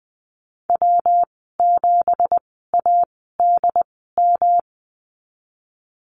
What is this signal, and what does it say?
Morse code, heard as a keyed tone, about 20 wpm, spaced as for 15 wpm, 700 Hz. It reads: W7ADM
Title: 186 Lesson 30g - Letter b - Call Signs 20wpm